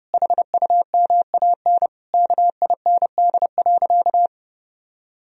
Morse code, heard as a keyed tone, about 30 wpm, 700 Hz. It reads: HUMAN KIND.
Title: Sentences from Top 1000 Words - Review Part 4 - 30wpm